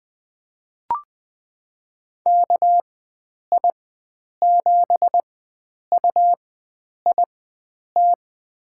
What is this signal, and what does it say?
Morse code, heard as a keyed tone, about 20 wpm, spaced as for 9 wpm, 700 Hz. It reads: KI7UIT
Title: USA Call Signs - 6 Characters - 4x Character Spacing - 20wpm